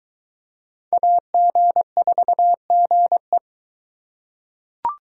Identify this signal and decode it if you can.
Morse code, heard as a keyed tone, about 22 wpm, 700 Hz. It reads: AG4GE